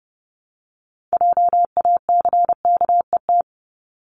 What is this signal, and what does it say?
Morse code, heard as a keyed tone, about 30 wpm, 700 Hz. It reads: JACKET
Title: Top 1751-2000 Words 30wpm